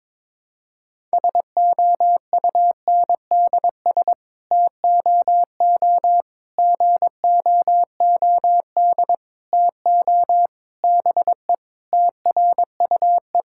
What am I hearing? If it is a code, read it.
Morse code, heard as a keyed tone, about 22 wpm, 700 Hz. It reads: SOUNDS TOO GOOD TO BE TRUE
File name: Sets of 9 Words - 22wpm